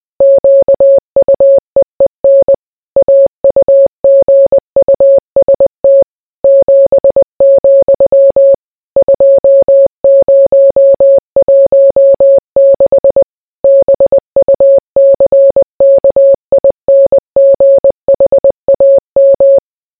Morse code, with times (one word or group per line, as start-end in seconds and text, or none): 0.2-2.5 QUEEN
3.0-6.0 AUGUST
6.4-8.5 7,
9.0-13.2 2016
13.6-19.6 BUCKINGHAM